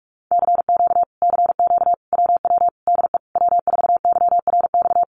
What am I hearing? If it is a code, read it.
Morse code, heard as a keyed tone, about 46 wpm, 700 Hz. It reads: CQ CQ WW DE W4YRX
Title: Common QSO Phrases Encore 45wpm